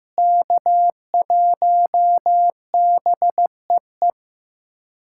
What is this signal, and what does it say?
Morse code, heard as a keyed tone, about 15 wpm, 700 Hz. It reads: K1BEE
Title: USA Callsign DE Callsign - 5 Characters - 15wpm